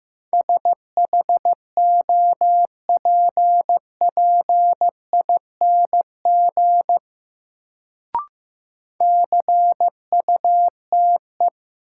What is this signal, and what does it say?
Morse code, heard as a keyed tone, about 15 wpm, 700 Hz. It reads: SHOPPING CUTE
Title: Sentences from Top 1000 Words - Review Part 7 - New Word Drill - 15wpm